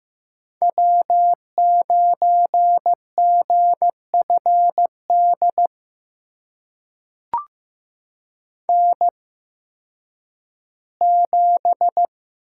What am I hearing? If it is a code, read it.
Morse code, heard as a keyed tone, about 15 wpm, spaced as for 10 wpm, 700 Hz. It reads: W9GFD N7
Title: USA Call Signs - 8x Character Spacing - 15wpm